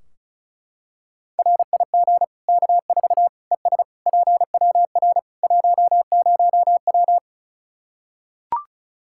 Morse code, heard as a keyed tone, about 36 wpm, 700 Hz. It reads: RIG K4 ES PWR 10W
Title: Practice QSO - Exchange 2 - Info - 35wpm